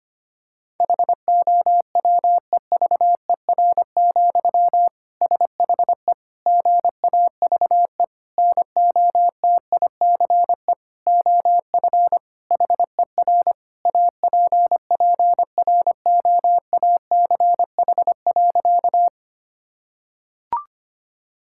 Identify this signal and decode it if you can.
Morse code, heard as a keyed tone, about 25 wpm, 700 Hz. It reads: HOWEVER, SHE GAVE NOTICE OF HER APPROACH.
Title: Sentences from Top 1000 Words - Review Part 6 - 25wpm